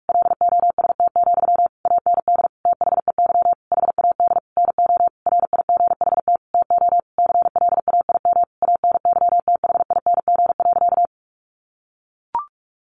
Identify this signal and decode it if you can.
Morse code, heard as a keyed tone, about 44 wpm, 700 Hz. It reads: LOST, AND THEY HAD NO RIGHT TO CLAIM ANYTHING.
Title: Sentences from Top 1500 Words 45wpm